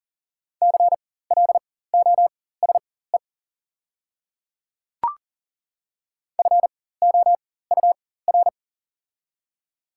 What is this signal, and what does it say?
Morse code, heard as a keyed tone, about 40 wpm, spaced as for 19 wpm, 700 Hz. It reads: CLOSE FOUR
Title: 4x Character Spacing - Top 300 Words - 40wpm